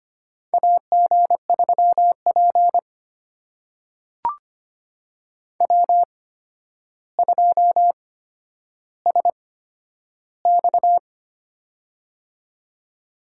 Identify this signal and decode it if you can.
Morse code, heard as a keyed tone, about 25 wpm, spaced as for 11 wpm, 700 Hz. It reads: AG3P W2SX
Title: USA Call Signs - 4 Characters - 8x Character Spacing - 25wpm